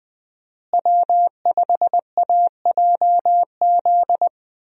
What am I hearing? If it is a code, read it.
Morse code, heard as a keyed tone, about 20 wpm, 700 Hz. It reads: W5AJZ
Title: USA Call Signs - 5 Characters - Repeated 3x - 20wpm